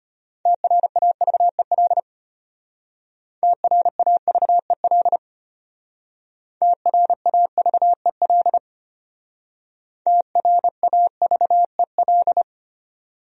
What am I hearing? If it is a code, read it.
Morse code, heard as a keyed tone, about 33 wpm, 700 Hz. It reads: TRAVEL TRAVEL TRAVEL TRAVEL